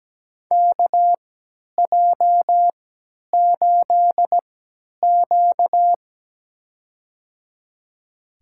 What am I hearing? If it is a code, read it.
Morse code, heard as a keyed tone, about 17 wpm, spaced as for 10 wpm, 700 Hz. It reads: KJ8Q
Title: USA Call Signs - 3x Character Spacing - 17wpm